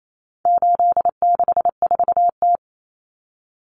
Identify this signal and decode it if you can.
Morse code, heard as a keyed tone, about 28 wpm, 700 Hz. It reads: 864T
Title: Common Cut Numbers - 4-Digit 28wpm